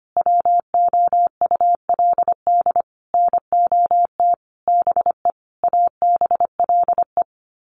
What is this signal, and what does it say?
Morse code, heard as a keyed tone, about 25 wpm, 700 Hz. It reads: WOULD NOT BE ABLE
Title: Sets of 5 Words Encore2 25wpm